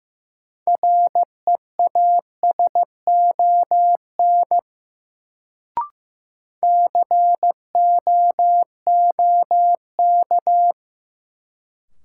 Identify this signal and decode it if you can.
Morse code, heard as a keyed tone, about 15 wpm, 700 Hz. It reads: REASON COOK
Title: Sentences from Top 700 Words - New Word Drill - 15wpm